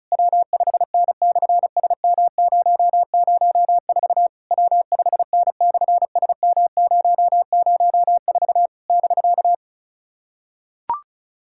Morse code, heard as a keyed tone, about 36 wpm, 700 Hz. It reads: W5N/SM004 W5N/SM004 <BK>
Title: SOTA QSO Practice 35wpm